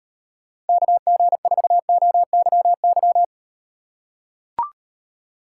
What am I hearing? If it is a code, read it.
Morse code, heard as a keyed tone, about 38 wpm, 700 Hz. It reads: KG4OYY